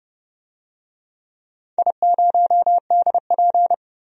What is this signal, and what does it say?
Morse code, heard as a keyed tone, about 28 wpm, 700 Hz. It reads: I0DP
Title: DXCC-100 Call Signs - 4 Characters - 30wpm